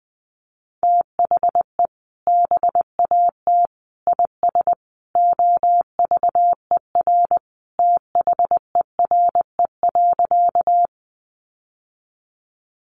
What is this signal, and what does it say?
Morse code, heard as a keyed tone, about 20 wpm, 700 Hz. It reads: THE BAT IS OVER THERE.
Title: Sentences from Top 1500 Words 20wpm